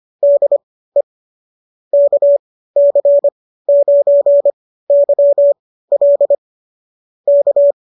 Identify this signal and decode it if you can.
Morse code, heard as a keyed tone, about 25 wpm, spaced as for 15 wpm, 590 Hz. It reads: DE KC9YL K